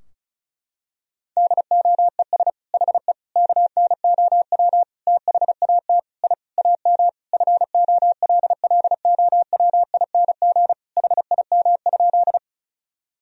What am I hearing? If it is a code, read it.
Morse code, heard as a keyed tone, about 34 wpm, 700 Hz. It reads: DOES HE KNOW THAT I AM FOLLOWING HIM?
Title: Sentences from Top 1000 Words - Review Part 4 - 35wpm